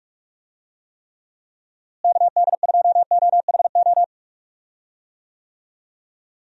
Morse code, heard as a keyed tone, about 41 wpm, 700 Hz. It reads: KD2OHO
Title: USA Call Signs - 6 Characters - Repeated 2x - 45wpm